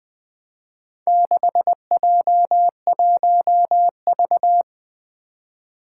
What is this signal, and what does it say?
Morse code, heard as a keyed tone, about 20 wpm, 700 Hz. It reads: 6J1V
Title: Four Challenging Letter-Number Combinations 20wpm